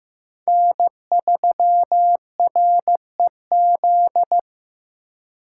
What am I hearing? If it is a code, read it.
Morse code, heard as a keyed tone, about 18 wpm, 700 Hz. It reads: N3REZ
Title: Speed Racing - USA 1x3 Call Signs 15wpm